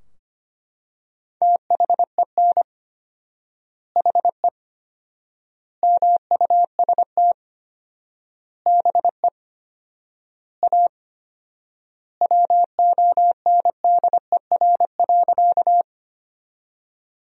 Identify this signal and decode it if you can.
Morse code, heard as a keyed tone, about 25 wpm, 700 Hz. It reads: THEN HE MUST BE A WONDER.
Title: 4x Word-Spacing - Sentences from Top 500 Words - 25wpm